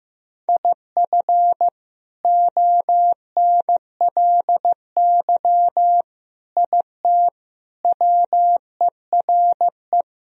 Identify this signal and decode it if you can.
Morse code, heard as a keyed tone, about 15 wpm, 700 Hz. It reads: IF ONLY IT WERE
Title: Sentences from Top 500 Words and Abbreviations 15wpm